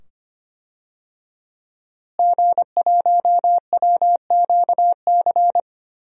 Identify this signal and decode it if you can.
Morse code, heard as a keyed tone, about 25 wpm, 700 Hz. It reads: G1WQC
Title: DXCC-100 Call Signs - 5 Characters - Repeated 2x - 25wpm